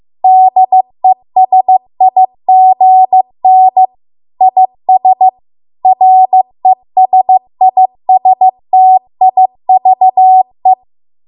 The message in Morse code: DESIGN IS RESISTIVE